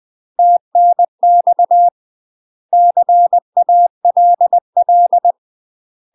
Morse code, heard as a keyed tone, about 20 wpm, 700 Hz. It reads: TNX CALL
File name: UK MASTER QSOs TRAINING / 20WPM / UK MASTER QSOs 0029